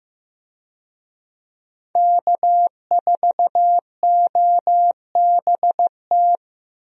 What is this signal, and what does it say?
Morse code, heard as a keyed tone, about 15 wpm, 700 Hz. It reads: K4OBT
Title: USA Call Signs - 5 Characters - Repeated 3x - 15wpm